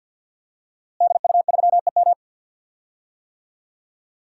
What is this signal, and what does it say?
Morse code, heard as a keyed tone, about 47 wpm, 700 Hz. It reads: DU3EM
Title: DXCC-100 Call Signs - 5 Characters - Repeated 3x - 50wpm